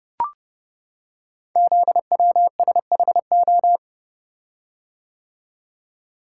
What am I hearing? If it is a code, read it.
Morse code, heard as a keyed tone, about 30 wpm, 700 Hz. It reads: ZWSHO
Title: Code Groups - 5 Letters - 30wpm